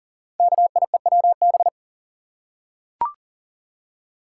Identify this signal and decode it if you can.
Morse code, heard as a keyed tone, about 42 wpm, 690 Hz. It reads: KIEWB
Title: Code Groups - 5 Letters - 40wpm